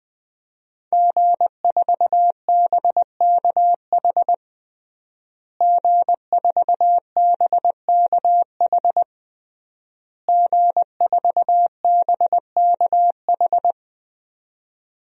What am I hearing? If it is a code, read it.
Morse code, heard as a keyed tone, about 20 wpm, 700 Hz. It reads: G4BKH G4BKH G4BKH
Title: DXCC-100 Call Signs - 5 Characters - Repeated 3x - 20wpm